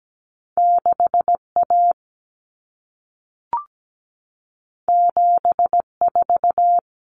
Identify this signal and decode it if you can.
Morse code, heard as a keyed tone, about 17 wpm, 700 Hz. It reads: 6A 74
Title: Common Cut Numbers - 2-Digit 17wpm